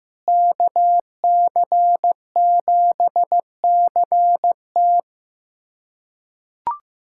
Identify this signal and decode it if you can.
Morse code, heard as a keyed tone, about 15 wpm, 700 Hz. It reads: KC7CT
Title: USA Callsign DE Callsign - 5 Characters - 15wpm